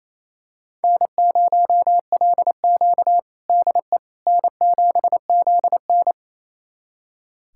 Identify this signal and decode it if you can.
Morse code, heard as a keyed tone, about 28 wpm, 700 Hz. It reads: N0LQ DE N7ZN
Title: USA Callsign DE Callsign - 4 Characters - 28wpm